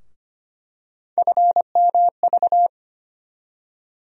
Morse code, heard as a keyed tone, about 25 wpm, 700 Hz. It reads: FMV
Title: USA Call Sign Postfixes - 3 Letters 25wpm